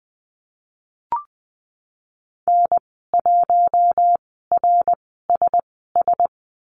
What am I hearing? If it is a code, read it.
Morse code, heard as a keyed tone, about 20 wpm, spaced as for 15 wpm, 700 Hz. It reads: N1RSS
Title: 055 Lesson 10g - Letter r - Call Signs 20wpm